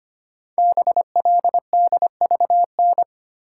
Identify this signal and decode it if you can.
Morse code, heard as a keyed tone, about 25 wpm, 700 Hz. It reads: BLDVN
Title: Code Groups - 5 Letter:Numbers - 25wpm